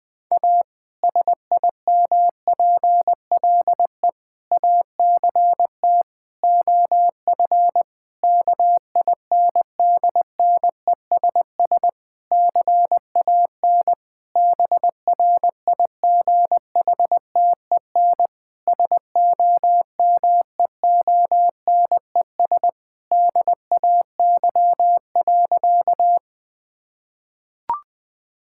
Morse code, heard as a keyed tone, about 19 wpm, 700 Hz. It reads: A SIMPLE ACT OF KINDNESS CAN BRIGHTEN SOMEONES DAY.
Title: Difficult Sentences with Hints - 20wpm